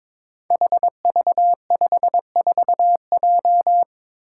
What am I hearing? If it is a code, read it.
Morse code, heard as a keyed tone, about 22 wpm, 700 Hz. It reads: HV54J
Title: Five Challenging Letter-Number Combinations 22wpm